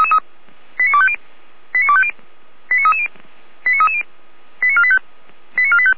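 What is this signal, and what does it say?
Five-tone selective call (ZVEI), heard as quick runs of five tones, tones of about 70 ms. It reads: 89250 89250 89330 89330 89485 89485
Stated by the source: ZVEI five-tone selective call received on SDR, from the audio server of feuerwehr-orlamuende.org (fire brigade Orlamünde).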